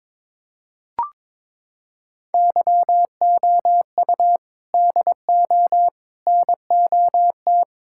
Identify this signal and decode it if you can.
Morse code, heard as a keyed tone, about 22 wpm, 700 Hz. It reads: YOU DO NOT
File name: practice / 22wpm / Sentences with 3-Word Phrases - 22wpm